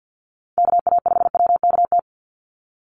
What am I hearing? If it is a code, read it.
Morse code, heard as a keyed tone, about 50 wpm, 700 Hz. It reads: KA5WKT
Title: USA Call Signs 50wpm